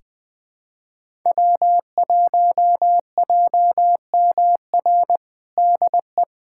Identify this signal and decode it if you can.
Morse code, heard as a keyed tone, about 20 wpm, 700 Hz. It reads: W1JMR DE
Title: USA Callsign DE Callsign 20wpm